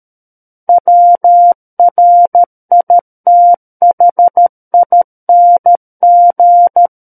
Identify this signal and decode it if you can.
Morse code, heard as a keyed tone, about 13 wpm, 700 Hz. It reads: WRITHING
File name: Book One / 04-Chapter 4 - 13 WPM